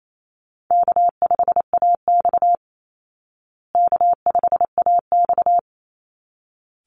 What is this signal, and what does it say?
Morse code, heard as a keyed tone, about 28 wpm, 700 Hz. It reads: K5AX K5AX
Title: Easy Call Signs Repeated 2x - 28wpm